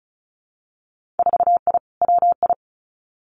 Morse code, heard as a keyed tone, about 36 wpm, 700 Hz. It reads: 4I WI
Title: Winter Field Day Exchanges 35wpm